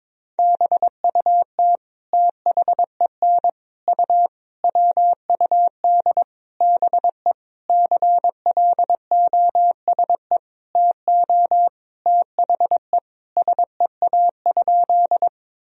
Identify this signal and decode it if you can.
Morse code, heard as a keyed tone, about 22 wpm, 700 Hz. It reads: BUT THEN U WUD BE CLOSE TO THE SEA?
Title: Sentences from Top 300 Words and Abbreviations 22wpm